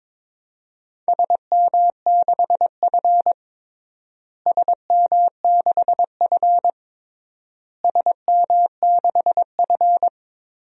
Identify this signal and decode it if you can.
Morse code, heard as a keyed tone, about 22 wpm, 700 Hz. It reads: SM6F SM6F SM6F
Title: DXCC-100 Call Signs - 4 Characters - Repeated 3x - 22wpm